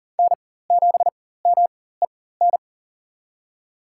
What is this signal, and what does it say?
Morse code, heard as a keyed tone, about 40 wpm, spaced as for 19 wpm, 700 Hz. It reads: N7MEN
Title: USA Call Signs - 4x Character Spacing - 40wpm